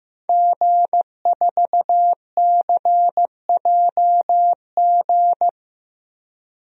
Morse code, heard as a keyed tone, about 15 wpm, 700 Hz. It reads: G4CJG
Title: DXCC-100 Call Signs - 5 Characters - Repeated 2x - 15wpm